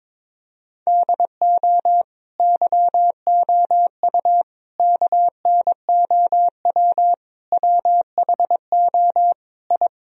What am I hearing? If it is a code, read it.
Morse code, heard as a keyed tone, about 22 wpm, 700 Hz. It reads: DO YOU KNOW WHO I
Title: Sentences from Top 1000 Words - Review Part 2 - 22wpm